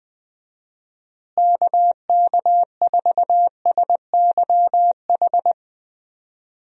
Morse code, heard as a keyed tone, about 21 wpm, 700 Hz. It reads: KK4SYH
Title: USA Call Signs - 6 Characters - Repeated 3x - 20wpm